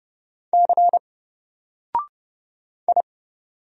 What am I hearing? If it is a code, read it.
Morse code, heard as a keyed tone, about 30 wpm, 700 Hz. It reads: C I
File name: Single-Letter-Number-30wpm